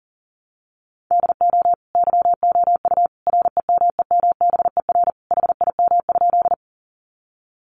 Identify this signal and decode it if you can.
Morse code, heard as a keyed tone, about 40 wpm, 700 Hz. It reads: DO YOU REMEMBER HIM?
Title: Sentences from Top 400 Words 40wpm